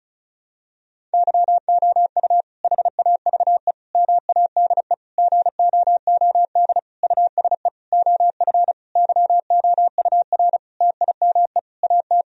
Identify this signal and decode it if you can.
Morse code, heard as a keyed tone, about 36 wpm, 700 Hz. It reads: YOU HAVE MADE GOOD USE OF YOUR TIME AT
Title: Sentences from Top 1000 Words - Review Part 1 - 35wpm